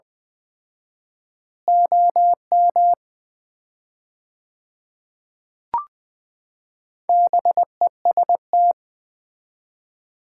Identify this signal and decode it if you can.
Morse code, heard as a keyed tone, about 21 wpm, 700 Hz. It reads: OM BEST
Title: Common QSO Phrases - 4x Word-Spacing - 20wpm